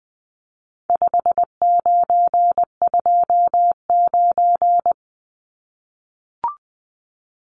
Morse code, heard as a keyed tone, about 20 wpm, 700 Hz. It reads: 5929
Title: Numbers - 4-Digit 20wpm